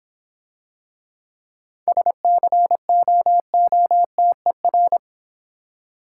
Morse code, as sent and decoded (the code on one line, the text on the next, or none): ... -.-. --- --- - . .-.
SCOOTER